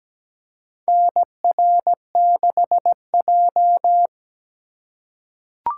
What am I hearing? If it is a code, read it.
Morse code, heard as a keyed tone, about 17 wpm, 700 Hz. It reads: NR6J
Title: USA Call Signs - 4 Characters - 3x Character Spacing - 17wpm